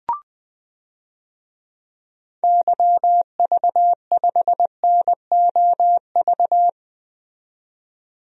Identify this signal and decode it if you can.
Morse code, heard as a keyed tone, about 20 wpm, 700 Hz. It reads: YV5NOV
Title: DXCC-100 Call Signs - 6 Characters - 20wpm